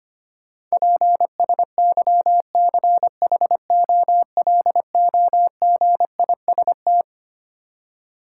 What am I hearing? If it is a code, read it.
Morse code, heard as a keyed tone, about 25 wpm, 700 Hz. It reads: PSYCHOLOGIST
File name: Twelve Letter Words - 25wpm